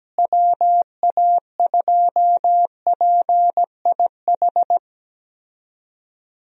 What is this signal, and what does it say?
Morse code, heard as a keyed tone, about 17 wpm, 700 Hz. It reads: WA2PIH
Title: USA Call Signs - 6 Characters - Repeated 3x - 17wpm